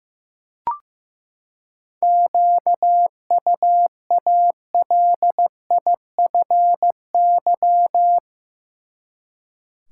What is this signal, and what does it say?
Morse code, heard as a keyed tone, about 15 wpm, 700 Hz. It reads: QUALIFY
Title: Seven Letter Words - 15wpm